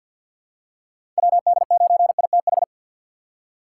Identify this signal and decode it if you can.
Morse code, heard as a keyed tone, about 51 wpm, 700 Hz. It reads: WD9ITH
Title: USA Call Signs - 6 Characters - 6x Character Spacing - 50wpm